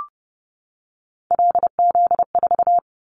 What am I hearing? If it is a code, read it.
Morse code, heard as a keyed tone, about 30 wpm, 700 Hz. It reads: LZ4
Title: Callsign Prefixes for Top 100 DXCC - 30wpm